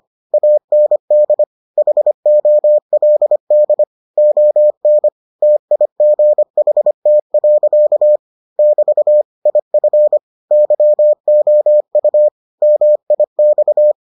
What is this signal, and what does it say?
Morse code, heard as a keyed tone, about 25 wpm, 600 Hz. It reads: AND HOLD ON TIGHT. <BT> IF YOU MIX